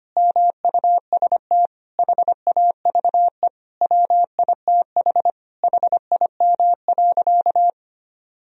Morse code, heard as a keyed tone, about 25 wpm, 700 Hz. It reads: MUST HAVE WITH HIM.
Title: Sentences from Top 1000 Words - Review Part 1 - 25wpm